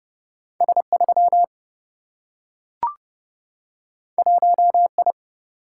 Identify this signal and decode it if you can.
Morse code, heard as a keyed tone, about 30 wpm, 700 Hz. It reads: S 3 1 I
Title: Single-Letter Single-Number 30wpm